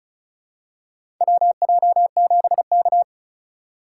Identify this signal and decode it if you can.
Morse code, heard as a keyed tone, about 36 wpm, 700 Hz. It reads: WJ7K
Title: USA Call Signs - 3x Character Spacing - 35wpm